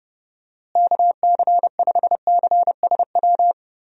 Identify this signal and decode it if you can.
Morse code, heard as a keyed tone, about 32 wpm, 700 Hz. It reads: KC5CSW
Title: USA Call Signs - 6 Characters - Repeated 3x - 30wpm